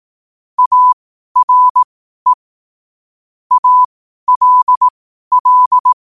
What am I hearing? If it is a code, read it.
Morse code, heard as a keyed tone, about 18 wpm, spaced as for 12 wpm, 1000 Hz. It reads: ARE ALL
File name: qotdfiles / 12wpm